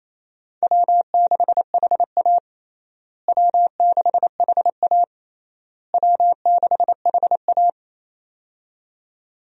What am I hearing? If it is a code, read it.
Morse code, heard as a keyed tone, about 27 wpm, 700 Hz. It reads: W6HA W6HA W6HA
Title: USA Call Signs - 4 Characters - Repeated 3x - 28wpm